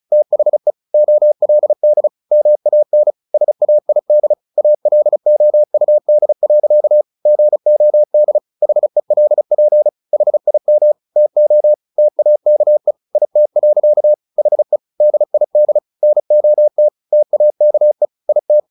Morse code, heard as a keyed tone, about 34 wpm, 600 Hz. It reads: THE OLD MAN SAID ALOUD. GOD HELP HIM TO TAKE IT. HE DID NOT TAKE IT
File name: the old man and the sea / the old man and the sea 1 35wpm